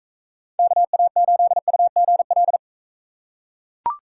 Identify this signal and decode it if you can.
Morse code, heard as a keyed tone, about 41 wpm, 700 Hz. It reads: KA8UGL